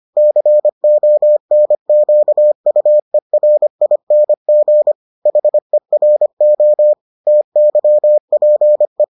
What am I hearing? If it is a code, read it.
Morse code, heard as a keyed tone, about 25 wpm, 600 Hz. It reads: CONQUERING HERO TYPE